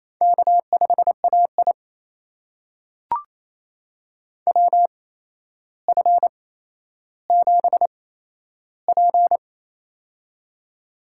Morse code, heard as a keyed tone, about 27 wpm, spaced as for 12 wpm, 700 Hz. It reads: K5AI WF7P
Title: USA Call Signs - 4 Characters - 8x Character Spacing - 28wpm